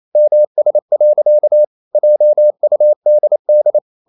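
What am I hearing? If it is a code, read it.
Morse code, heard as a keyed tone, about 28 wpm, 600 Hz. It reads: MS. JUDD
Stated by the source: Morse code practice at 28 wpm, text of Ken Follett's novel Never in English, from CW med Gnister.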